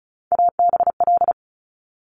Morse code, heard as a keyed tone, about 36 wpm, 700 Hz. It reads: ABL
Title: Top Three Letter Combinations in Words 35wpm